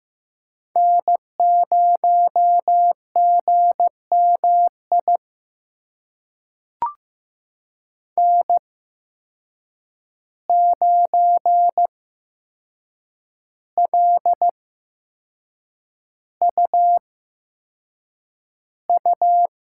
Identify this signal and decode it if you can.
Morse code, heard as a keyed tone, about 15 wpm, spaced as for 6 wpm, 700 Hz. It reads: N0GMI N9LUU
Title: USA Call Signs - 5 Characters - 8x Character Spacing - 15wpm